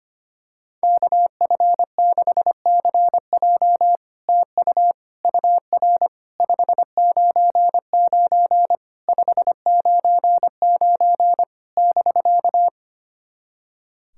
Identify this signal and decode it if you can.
Morse code, heard as a keyed tone, about 25 wpm, 700 Hz. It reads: KF6CJ TU UR 599 599 <BK>
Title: SOTA QSO Practice 25wpm